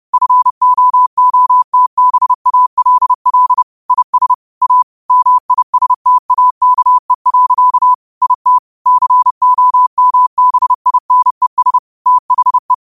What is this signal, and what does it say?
Morse code, heard as a keyed tone, about 30 wpm, 1000 Hz. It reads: FOOTBALL IS A MISTAKE. IT COMBINES THE
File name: qotdfiles / 30wpm